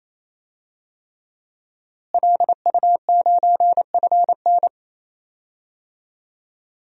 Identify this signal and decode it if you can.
Morse code, heard as a keyed tone, about 28 wpm, 700 Hz. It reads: LU9FN
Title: DXCC-100 Call Signs - 5 Characters - Repeated 3x - 28wpm